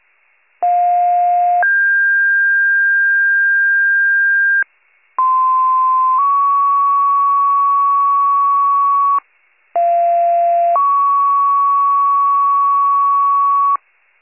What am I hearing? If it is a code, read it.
Two-tone page, heard as a one-second tone, then a three-second tone, about 710 then 1600 Hz, about 1040 then 1120 Hz, about 690 then 1120 Hz. It is heard from 0.6 to 13.8 s.
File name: pager / 46 RESCUE DUTY / wcalert